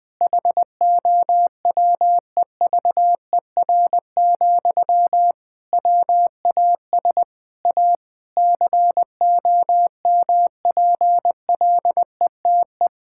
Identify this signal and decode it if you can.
Morse code, heard as a keyed tone, about 21 wpm, 700 Hz. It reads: HOWEVER, WAS A COMPLETE
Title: Sentences from Top 1000 Words - Review Part 6 - 20wpm